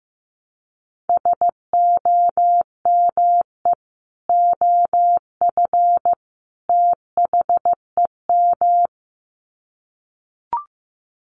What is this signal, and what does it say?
Morse code, heard as a keyed tone, about 15 wpm, 700 Hz. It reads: SOME OF THEM
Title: Sets of 3 Words 15wpm